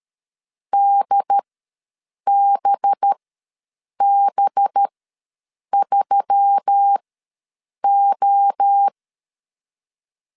Morse code, heard as a keyed tone, about 13 wpm, spaced as for 7 wpm, 800 Hz. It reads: DBB3O
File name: 5 LB ws 12wpm